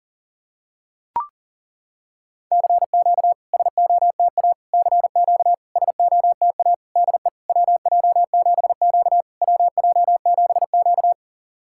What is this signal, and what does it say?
Morse code, heard as a keyed tone, about 41 wpm, 700 Hz. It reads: CQ SOTA CQ SOTA DE WJ7Q WJ7Q
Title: SOTA QSO Practice 40wpm